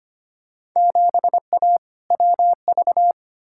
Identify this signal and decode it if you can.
Morse code, heard as a keyed tone, about 25 wpm, 700 Hz. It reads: 7A WV
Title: Field Day Exchanges Encore - 25wpm